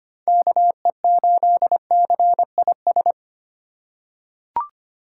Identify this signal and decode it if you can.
Morse code, heard as a keyed tone, about 25 wpm, 700 Hz. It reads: KE8CIS
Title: USA Call Signs - 6 Characters - 4x Character Spacing - 25wpm